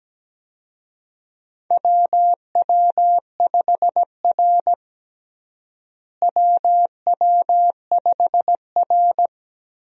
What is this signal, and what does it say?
Morse code, heard as a keyed tone, about 17 wpm, 700 Hz. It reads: WW5R WW5R
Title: USA Call Signs - 4 Characters - Repeated 2x - 17wpm